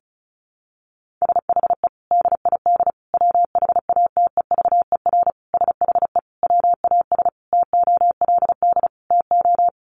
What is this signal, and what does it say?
Morse code, heard as a keyed tone, about 34 wpm, 700 Hz. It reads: SHE DID WHATEVER SHE WAS TOLD TO
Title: Sentences from Top 1100 Words 35wpm